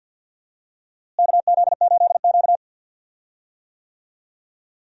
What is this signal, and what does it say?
Morse code, heard as a keyed tone, about 50 wpm, 700 Hz. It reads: KZ8Q
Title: USA Call Signs - 4 Characters - Repeated 2x - 50wpm